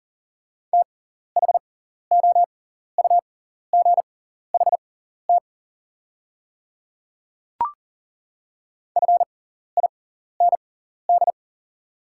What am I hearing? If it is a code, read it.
Morse code, heard as a keyed tone, about 37 wpm, spaced as for 14 wpm, 700 Hz. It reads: THOUGHT FIND
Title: 6x Character Spacing - Top 300 Words - 40wpm